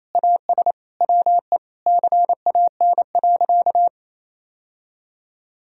Morse code, heard as a keyed tone, about 28 wpm, 700 Hz. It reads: AS WE CAN.
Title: Sentences from Top 1000 Words - Review Part 3 - 28wpm